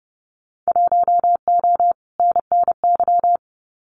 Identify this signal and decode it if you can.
Morse code, heard as a keyed tone, about 30 wpm, 700 Hz. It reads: 1O NNY
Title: Winter Field Day Exchanges 30wpm